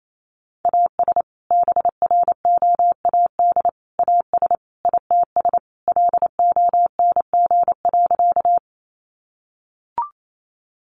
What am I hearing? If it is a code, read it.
Morse code, heard as a keyed tone, about 28 wpm, 700 Hz. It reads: AS BROAD AS ITS LONG.
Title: Sentences from Top 1000 Words 28wpm